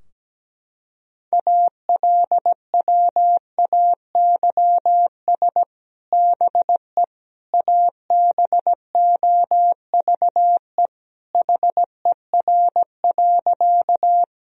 Morse code, heard as a keyed tone, about 17 wpm, 700 Hz. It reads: ALWAYS BE ABOVE HER.
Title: Sentences from Top 1000 Words - Review Part 2 - 17wpm